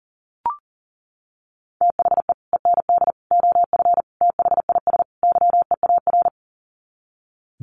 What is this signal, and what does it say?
Morse code, heard as a keyed tone, about 40 wpm, 700 Hz. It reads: THE END OF THIS YEAR
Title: Sets of 5 Words Encore 40wpm